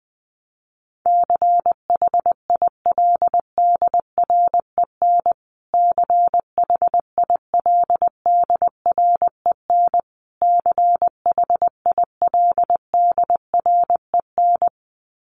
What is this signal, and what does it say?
Morse code, heard as a keyed tone, about 20 wpm, 700 Hz. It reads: CHILDREN CHILDREN CHILDREN
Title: Top 500 Words Repeated 20wpm